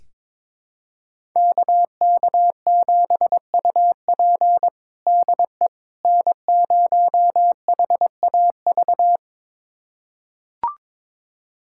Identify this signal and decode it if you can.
Morse code, heard as a keyed tone, about 22 wpm, 700 Hz. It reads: KK7UP DE N0HAV
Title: USA Callsign DE Callsign - 5 Characters - 22wpm